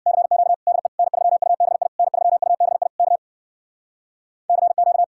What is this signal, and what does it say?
Morse code, heard as a keyed tone, about 64 wpm, 700 Hz. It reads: CQ DE N3UBI N3UBI K CQ